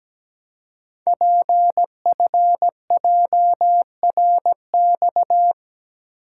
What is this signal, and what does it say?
Morse code, heard as a keyed tone, about 17 wpm, 700 Hz. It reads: PFJRX
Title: Code Groups - 5 Letters - 17wpm